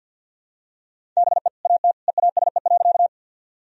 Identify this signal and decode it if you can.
Morse code, heard as a keyed tone, about 52 wpm, 700 Hz. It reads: BE AT EASE.
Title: Sentences from Top 1000 Words - Review Part 1 - 50wpm